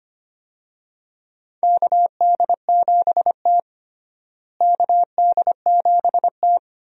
Morse code, heard as a keyed tone, about 25 wpm, 700 Hz. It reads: KD7T KD7T
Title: USA Call Signs - 4 Characters - Repeated 2x - 25wpm